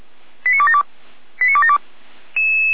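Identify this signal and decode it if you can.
Five-tone selective call (ZVEI), heard as quick runs of five tones, tones of about 70 ms. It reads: 89272 89272
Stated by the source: Five-tone ZVEI selective call received by SDR, from feuerwehr-orlamuende.org fire brigade audio archive.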